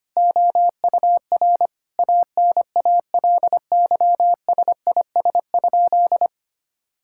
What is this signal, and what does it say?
Morse code, heard as a keyed tone, about 25 wpm, 700 Hz. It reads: OUR ANALYSIS?
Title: Sentences from Top 1000 Words - Review Part 5 - 25wpm